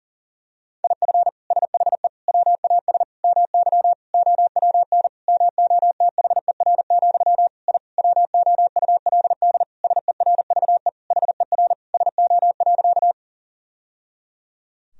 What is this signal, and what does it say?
Morse code, heard as a keyed tone, about 40 wpm, 700 Hz. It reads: IF SHE WAS MY OWN MOTHER, I WOULD SERVE HER SO.
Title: Sentences from Top 1000 Words - Review Part 2 - 40wpm